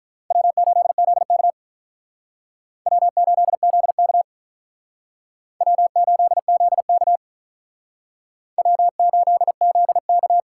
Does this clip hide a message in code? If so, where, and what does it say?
Yes: Morse code from 0.3 to 10.4 s, heard as a keyed tone, about 46 wpm, 700 Hz. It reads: W8ZK W8ZK W8ZK W8ZK